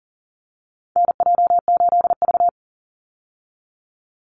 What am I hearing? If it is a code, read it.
Morse code, heard as a keyed tone, about 40 wpm, 700 Hz. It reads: NJ8V
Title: Call Signs Repeated 2x - 40wpm